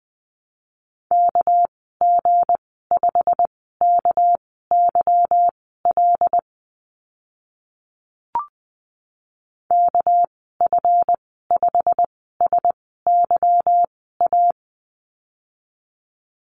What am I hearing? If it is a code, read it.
Morse code, heard as a keyed tone, about 20 wpm, spaced as for 15 wpm, 700 Hz. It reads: KG5KYL KF5SYA
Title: 200 Lesson 32g - Letter k - Call Signs 20wpm